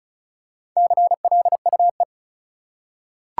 Morse code, heard as a keyed tone, about 36 wpm, 700 Hz. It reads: CLUE
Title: Four Letter Words - 35wpm